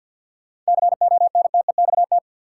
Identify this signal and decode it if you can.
Morse code, heard as a keyed tone, about 52 wpm, 700 Hz. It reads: CONTEXT
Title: Sets of 4 Words - New Word Drill - 50wpm